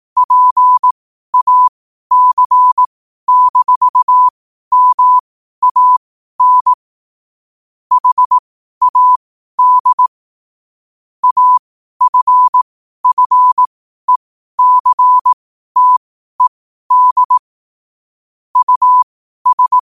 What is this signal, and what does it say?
Morse code, heard as a keyed tone, about 18 wpm, spaced as for 12 wpm, 1000 Hz. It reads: PAC-MAN HAD AFFECTED US
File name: qotdfiles / 12wpm